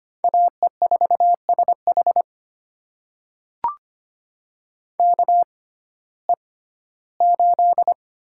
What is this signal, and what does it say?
Morse code, heard as a keyed tone, about 25 wpm, spaced as for 15 wpm, 700 Hz. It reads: AE4SH KE8
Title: USA Call Signs - 5 Characters - 6x Character Spacing - 25wpm